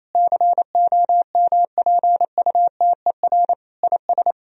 Morse code, heard as a keyed tone, about 28 wpm, 700 Hz. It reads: COMPUTER IS
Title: Sentences from Top 1000 Words - Review Part 3 - 28wpm